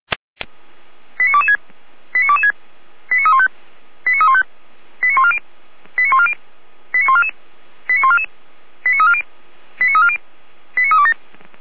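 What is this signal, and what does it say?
Five-tone selective call (ZVEI), heard as quick runs of five tones, tones of about 70 ms. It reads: 89226 89226 89315 89315 89140 89140 89144 89144 89340 89340 89327
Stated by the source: ZVEI five-tone call 89235 received by SDR, from the feuerwehr-orlamuende.org fire brigade archive.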